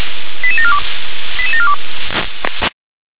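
Five-tone selective call (ZVEI), heard as quick runs of five tones, tones of about 70 ms. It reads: 88742 88742
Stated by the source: ZVEI five-tone selective call received by SDR.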